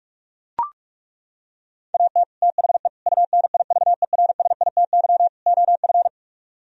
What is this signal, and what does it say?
Morse code, heard as a keyed tone, about 34 wpm, 700 Hz. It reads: AT THE UNIVERSITY OF
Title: Speed-Racing - Sets of 4 Words 30wpm